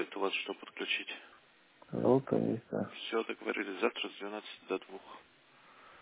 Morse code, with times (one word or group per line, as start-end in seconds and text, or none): none